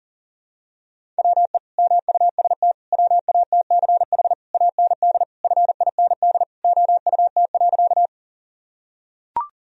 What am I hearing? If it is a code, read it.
Morse code, heard as a keyed tone, about 40 wpm, 700 Hz. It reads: WE MUST WATCH AND FIND OUT.
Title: Sentences from Top 1000 Words - Review Part 1 - 40wpm